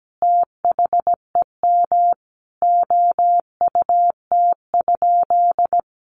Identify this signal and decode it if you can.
Morse code, heard as a keyed tone, about 17 wpm, 700 Hz. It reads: THEM OUT?
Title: Sentences from Top 300 Words 17wpm